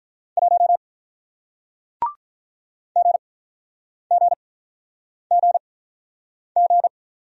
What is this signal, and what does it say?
Morse code, heard as a keyed tone, about 48 wpm, spaced as for 9 wpm, 700 Hz. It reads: 1 GGGG